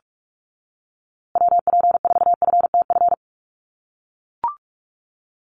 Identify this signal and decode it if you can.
Morse code, heard as a keyed tone, about 44 wpm, 700 Hz. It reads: WP4FTF
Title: USA Call Signs Repeated 3x 45wpm